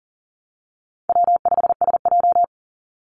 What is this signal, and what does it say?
Morse code, heard as a keyed tone, about 40 wpm, 700 Hz. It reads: W5SJ
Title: Call Signs Repeated 2x - 40wpm